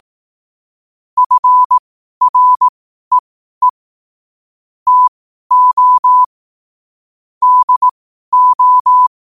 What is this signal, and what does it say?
Morse code, heard as a keyed tone, about 18 wpm, spaced as for 12 wpm, 1000 Hz. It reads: FREE TO DO
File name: qotdfiles / 12wpm